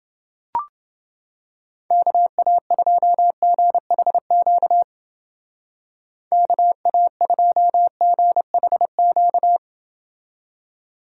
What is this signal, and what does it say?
Morse code, heard as a keyed tone, about 29 wpm, 700 Hz. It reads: KA2GHQ KA2GHQ